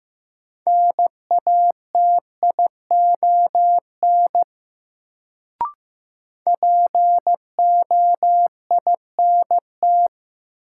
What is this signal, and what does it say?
Morse code, heard as a keyed tone, about 15 wpm, 700 Hz. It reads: NATION POINT
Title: Sets of 3 Words - New Word Drill - 15wpm